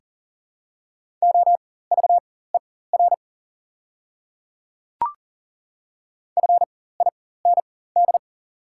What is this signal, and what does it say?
Morse code, heard as a keyed tone, about 39 wpm, spaced as for 19 wpm, 700 Hz. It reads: OVER FIND
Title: 4x Character Spacing - Top 100 Words - 40wpm